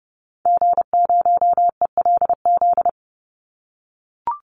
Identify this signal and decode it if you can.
Morse code, heard as a keyed tone, about 30 wpm, 700 Hz. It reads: G0ELZ
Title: Call Signs - 30wpm